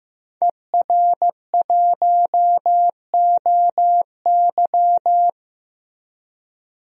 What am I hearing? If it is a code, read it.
Morse code, heard as a keyed tone, about 15 wpm, 700 Hz. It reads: ER1OY
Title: DXCC-100 Call Signs - 5 Characters - 15wpm